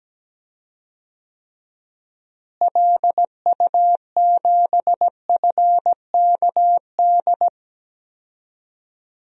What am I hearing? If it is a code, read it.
Morse code, heard as a keyed tone, about 17 wpm, 700 Hz. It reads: LU7FKD
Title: DXCC-100 Call Signs - 6 Characters - Repeated 2x - 17wpm